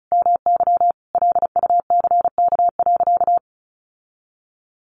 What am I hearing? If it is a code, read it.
Morse code, heard as a keyed tone, about 34 wpm, 700 Hz. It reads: MY LUCK.
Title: Sentences from Top 1500 Words 35wpm